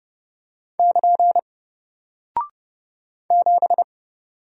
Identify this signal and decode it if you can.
Morse code, heard as a keyed tone, about 30 wpm, 700 Hz. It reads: <KN> 7
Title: Single Character (Letter:Number:Punctuation:Prosign) Alt 30wpm